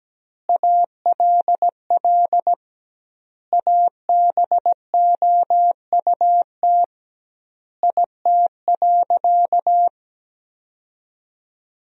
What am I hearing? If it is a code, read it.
Morse code, heard as a keyed tone, about 17 wpm, 700 Hz. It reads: ALL ABOUT IT.
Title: 2x Word-Spacing - Sentences from Top 400 Words - 17wpm